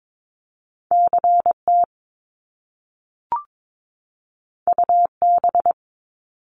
Morse code, heard as a keyed tone, about 22 wpm, 700 Hz. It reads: CT UB
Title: Top Two Letter Combinations in Words 22wpm